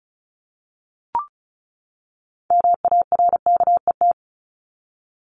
Morse code, heard as a keyed tone, about 34 wpm, 700 Hz. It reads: MARKET
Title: Top 501-750 Words 35wpm